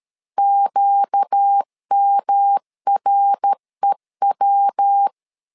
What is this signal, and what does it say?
Morse code, heard as a keyed tone, about 13 wpm, 800 Hz. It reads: QMREW
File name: LEZIONI / 4 Q 12wpm 1833